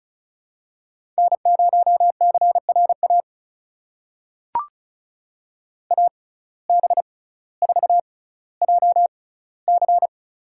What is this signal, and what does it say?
Morse code, heard as a keyed tone, about 36 wpm, spaced as for 18 wpm, 700 Hz. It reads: N0CRA AB4JC
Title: USA Call Signs - 5 Characters - 6x Character Spacing - 35wpm